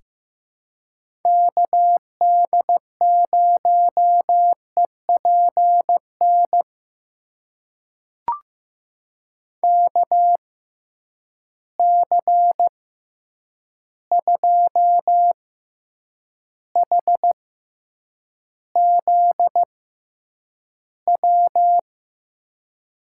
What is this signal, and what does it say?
Morse code, heard as a keyed tone, about 15 wpm, spaced as for 8 wpm, 700 Hz. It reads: KD0EPN KC2HZW
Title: USA Call Signs - 6 Characters - 6x Character Spacing - 15wpm